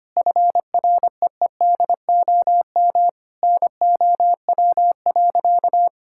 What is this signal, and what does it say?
Morse code, heard as a keyed tone, about 25 wpm, 700 Hz. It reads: FREEDOM NOW.
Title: Sentences from Top 1000 Words - Review Part 5 - 25wpm